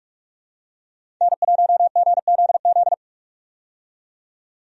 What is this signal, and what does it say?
Morse code, heard as a keyed tone, about 43 wpm, 700 Hz. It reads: N1GZZ
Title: USA Call Signs - 5 Characters - 45wpm